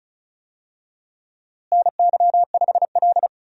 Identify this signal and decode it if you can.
Morse code, heard as a keyed tone, about 34 wpm, 700 Hz. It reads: NY5L
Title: USA Call Signs - 4 Characters - Repeated 2x - 35wpm